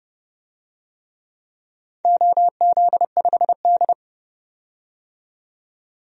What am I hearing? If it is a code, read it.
Morse code, heard as a keyed tone, about 29 wpm, 700 Hz. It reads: OZ5D
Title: DXCC-100 Call Signs - 4 Characters - 30wpm